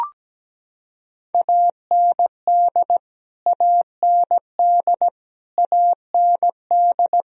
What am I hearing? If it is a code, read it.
Morse code, heard as a keyed tone, about 17 wpm, 700 Hz. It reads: AND AND AND
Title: Top 100 Words Repeated 17wpm